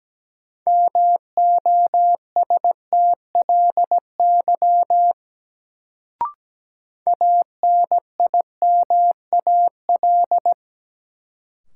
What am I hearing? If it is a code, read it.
Morse code, heard as a keyed tone, about 17 wpm, 700 Hz. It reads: MOSTLY ANIMAL
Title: Sentences from Top 1300 Words - New Word Drill Alt - 17wpm